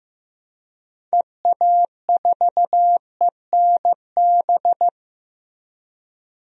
Morse code, heard as a keyed tone, about 15 wpm, 700 Hz. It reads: EA4ENB
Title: DXCC-100 Call Signs - 6 Characters - Repeated 3x - 15wpm